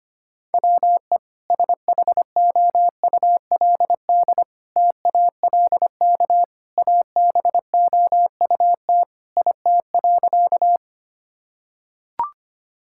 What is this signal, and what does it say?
Morse code, heard as a keyed tone, about 25 wpm, 700 Hz. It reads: WE SHOULD TALK ABOUT IT.
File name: practice / 25wpm / Sentences with 2-Word Phrases - 25wpm